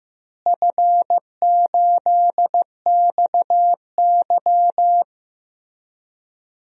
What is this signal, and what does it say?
Morse code, heard as a keyed tone, about 15 wpm, 700 Hz. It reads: F8XY
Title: DXCC-100 Call Signs - 4 Characters - Repeated 2x - 15wpm